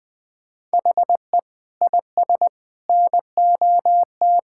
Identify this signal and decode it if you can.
Morse code, heard as a keyed tone, about 19 wpm, 700 Hz. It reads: HE IS NOT
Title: Sentences from Top 1000 Words - Review Part 5 - 20wpm